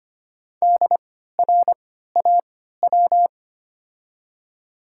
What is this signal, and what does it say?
Morse code, heard as a keyed tone, about 25 wpm, spaced as for 14 wpm, 700 Hz. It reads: DRAW
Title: 3x Character Spacing - Top 300 Words - 25wpm